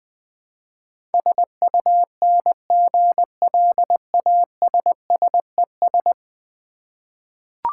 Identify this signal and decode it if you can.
Morse code, heard as a keyed tone, about 20 wpm, 700 Hz. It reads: SUNGLASSES
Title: Easy Words with Hints - 20wpm